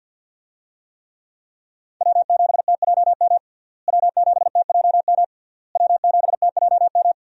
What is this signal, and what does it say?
Morse code, heard as a keyed tone, about 51 wpm, 700 Hz. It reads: W7TJM W7TJM W7TJM
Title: USA Call Signs - 5 Characters - Repeated 3x - 50wpm